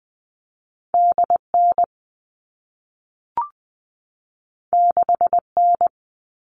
Morse code, heard as a keyed tone, about 20 wpm, 700 Hz. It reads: DN 6N
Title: All Cut Numbers - 2-Digit 20wpm